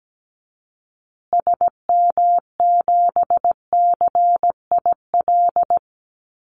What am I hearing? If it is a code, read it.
Morse code, heard as a keyed tone, about 17 wpm, 700 Hz. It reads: SM7CIL
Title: Call Signs Repeated 2x - 17wpm